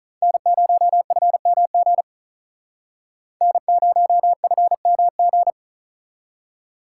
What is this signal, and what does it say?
Morse code, heard as a keyed tone, about 43 wpm, 700 Hz. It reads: N0FMG N0FMG
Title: Speed Racing - USA 1x3 Call Signs 35wpm